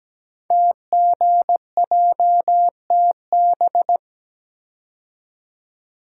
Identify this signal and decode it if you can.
Morse code, heard as a keyed tone, about 17 wpm, 700 Hz. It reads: TGJTB
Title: Code Groups - 5 Letters - 17wpm